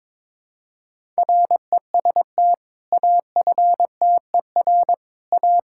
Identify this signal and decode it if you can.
Morse code, heard as a keyed tone, about 22 wpm, 700 Hz. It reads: REST AFTER A
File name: practice / 22wpm / Sentences with 2-Word Phrases - 22wpm